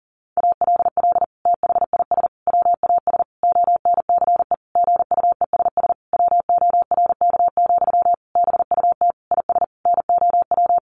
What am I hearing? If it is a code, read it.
Morse code, heard as a keyed tone, about 40 wpm, 700 Hz. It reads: ALL THIS WAS ONCE GUESS WORK, BUT IS NOW
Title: Sentences from Top 1500 Words 40wpm